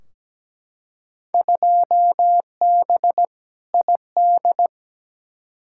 Morse code, heard as a keyed tone, about 17 wpm, 700 Hz. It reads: 2B ID
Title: Field Day Exchanges Encore - 17wpm